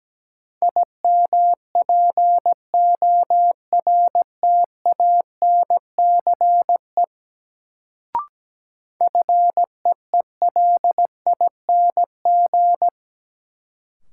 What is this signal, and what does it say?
Morse code, heard as a keyed tone, about 17 wpm, 700 Hz. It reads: IMPORTANCE FEELING
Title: Sentences from Top 1200 Words - New Word Drill Alt - 17wpm